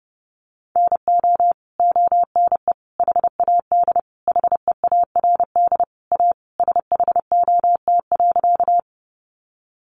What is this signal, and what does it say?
Morse code, heard as a keyed tone, about 30 wpm, 700 Hz. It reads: NO ONE HAD HEARD A SHOT.
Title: Sentences from Top 1100 Words 30wpm